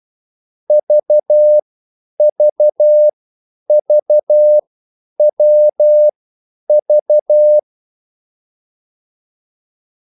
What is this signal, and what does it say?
Morse code, heard as a keyed tone, about 12 wpm, spaced as for 9 wpm, 600 Hz. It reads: VVVWV
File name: cwbook